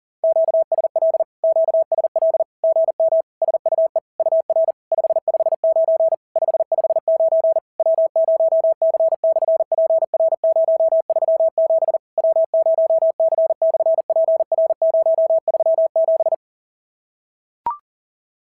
Morse code, heard as a keyed tone, about 40 wpm, 650 Hz. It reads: QSL QSL GM SUE UR 559 559 W0C/PR037 W0C/PR037
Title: SOTA QSO Practice 40wpm